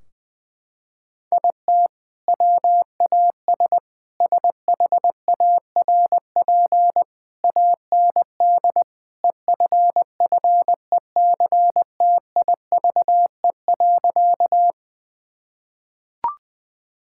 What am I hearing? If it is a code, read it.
Morse code, heard as a keyed tone, about 20 wpm, 700 Hz. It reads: IT WAS SHARP AND EFFECTIVE.
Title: Sentences from Top 1600 Words 20wpm